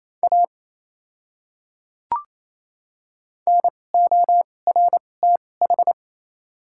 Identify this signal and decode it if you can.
Morse code, heard as a keyed tone, about 28 wpm, spaced as for 20 wpm, 700 Hz. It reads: A NORTH
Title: 2x Character Spacing - Top 300 Words - 28wpm